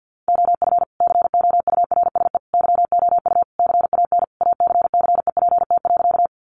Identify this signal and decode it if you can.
Morse code, heard as a keyed tone, about 50 wpm, 700 Hz. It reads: OF COURSE YOU CAN ACCEPT.
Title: Sentences from Top 1300 Words 50wpm